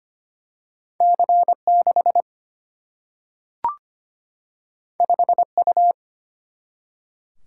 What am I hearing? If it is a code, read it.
Morse code, heard as a keyed tone, about 25 wpm, 700 Hz. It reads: C 6 5 U
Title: Single-Letter Single-Number 25wpm